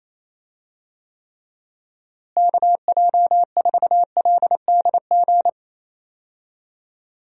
Morse code, heard as a keyed tone, about 28 wpm, 700 Hz. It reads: KJ4LDG
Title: USA Call Signs - 6 Characters - Repeated 3x - 28wpm